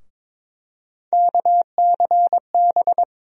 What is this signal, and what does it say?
Morse code, heard as a keyed tone, about 21 wpm, 700 Hz. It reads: KCB
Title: USA Call Sign Postfixes - 3 Letters 22wpm